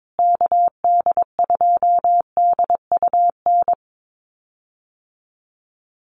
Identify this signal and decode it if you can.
Morse code, heard as a keyed tone, about 22 wpm, 700 Hz. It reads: KD2DUN
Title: USA Call Signs Repeated 2x 22wpm